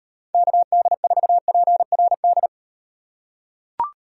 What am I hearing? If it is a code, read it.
Morse code, heard as a keyed tone, about 38 wpm, 700 Hz. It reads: KD4PRD